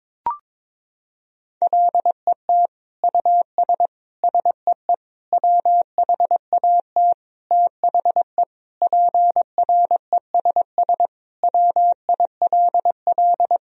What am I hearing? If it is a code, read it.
Morse code, heard as a keyed tone, about 22 wpm, 700 Hz. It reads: LET US SEE WHAT THE PRESS WILL
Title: Sentences from Top 1000 Words - Review Part 1 - 22wpm